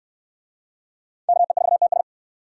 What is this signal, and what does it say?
Morse code, heard as a keyed tone, about 61 wpm, 700 Hz. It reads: KE4TS